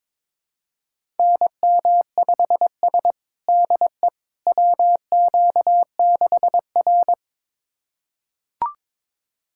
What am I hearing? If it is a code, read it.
Morse code, heard as a keyed tone, about 22 wpm, 700 Hz. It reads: NM5S DE WQ6R
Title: USA Callsign DE Callsign - 4 Characters - 22wpm